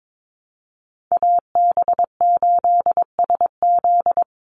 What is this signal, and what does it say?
Morse code, heard as a keyed tone, about 22 wpm, 700 Hz. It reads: AB8SZ
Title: USA Call Signs 22wpm